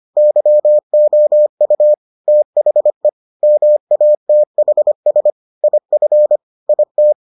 Morse code, heard as a keyed tone, about 25 wpm, 600 Hz. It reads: YOU THE MATHS IF IT